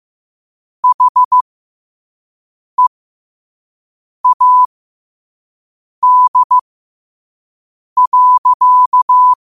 Morse code, heard as a keyed tone, about 15 wpm, spaced as for 5 wpm, 1000 Hz. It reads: HEAD.
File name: qotdfiles / 5wpm